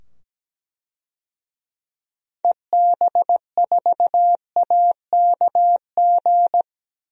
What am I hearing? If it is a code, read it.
Morse code, heard as a keyed tone, about 17 wpm, 700 Hz. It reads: EB4AKG
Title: DXCC-100 Call Signs - 6 Characters - 17wpm